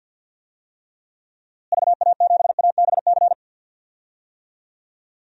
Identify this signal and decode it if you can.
Morse code, heard as a keyed tone, about 52 wpm, 700 Hz. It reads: VA7ABC
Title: DXCC-100 Call Signs - 6 Characters - Repeated 2x - 50wpm